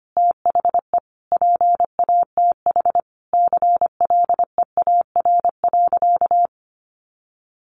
Morse code, heard as a keyed tone, about 25 wpm, 700 Hz. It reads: THE PATH CLEAR.
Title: Sentences from Top 1400 Words 25wpm